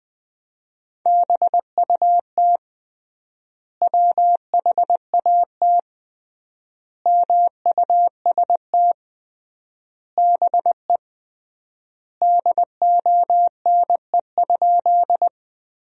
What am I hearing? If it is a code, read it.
Morse code, heard as a keyed tone, about 20 wpm, 700 Hz. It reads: BUT WHAT MUST BE DONE?
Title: 3x Word-Spacing - Sentences from Top 500 Words - 20wpm